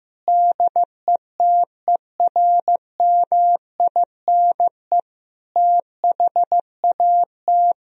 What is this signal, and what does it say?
Morse code, heard as a keyed tone, about 15 wpm, 700 Hz. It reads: DETERMINE THAT
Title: Sentences from Top 1000 Words - Review Part 4 - 15wpm